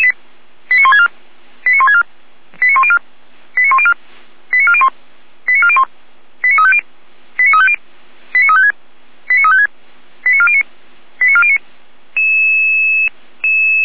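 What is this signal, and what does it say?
Five-tone selective call (ZVEI), heard as quick runs of five tones, tones of about 80 ms. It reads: 89164 89164 89104 89104 89401 89401 89350 89350 89356 89356 89409 89409
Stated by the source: ZVEI five-tone call 89080 received on an SDR.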